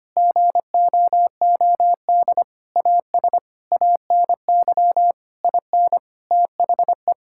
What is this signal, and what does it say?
Morse code, heard as a keyed tone, about 25 wpm, 700 Hz. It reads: GOOD AS ANY IN THE
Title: Sentences from Top 1600 Words 25wpm